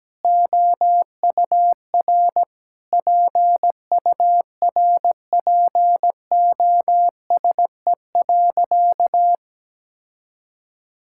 Morse code, heard as a keyed tone, about 17 wpm, 700 Hz. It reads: OUR PURPOSE.
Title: Sentences from Top 1600 Words 17wpm